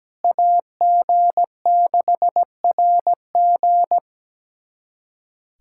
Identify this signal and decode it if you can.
Morse code, heard as a keyed tone, about 17 wpm, 700 Hz. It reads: AG6RG
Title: USA Callsign DE Callsign - 5 Characters - 17wpm